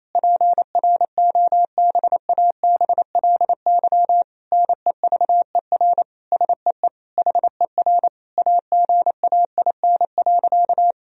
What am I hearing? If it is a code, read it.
Morse code, heard as a keyed tone, about 27 wpm, 700 Hz. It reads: PROBABLY NEVER SEE HER AGAIN.
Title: Sentences from Top 1000 Words - Review Part 4 - 28wpm